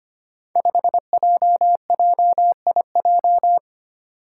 Morse code, heard as a keyed tone, about 25 wpm, 700 Hz. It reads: 5JJIJ
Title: Five Challenging Letter-Number Combinations 25wpm